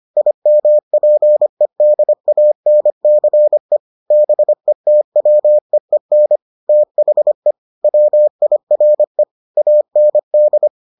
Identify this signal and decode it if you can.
Morse code, heard as a keyed tone, about 25 wpm, 600 Hz. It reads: IMPEDANCE BETWEEN THE WIRE AND